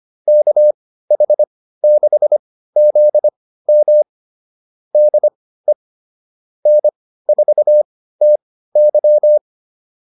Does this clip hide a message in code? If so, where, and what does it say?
Yes: Morse code from 0.3 to 9.4 s, heard as a keyed tone, about 25 wpm, spaced as for 15 wpm, 600 Hz. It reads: KH6ZM DE N4TY